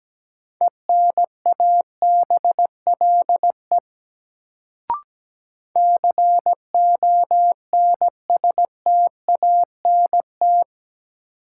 Morse code, heard as a keyed tone, about 17 wpm, 700 Hz. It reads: ENABLE CONSTANT
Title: Sentences from Top 1500 Words - New Word Drill - 17wpm